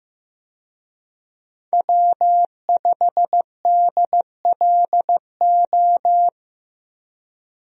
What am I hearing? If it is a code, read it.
Morse code, heard as a keyed tone, about 15 wpm, 700 Hz. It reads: W5DLO
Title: USA Call Signs - 5 Characters - 15wpm